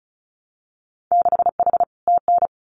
Morse code, heard as a keyed tone, about 36 wpm, 700 Hz. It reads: 6H TN
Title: Winter Field Day Exchanges Encore 35wpm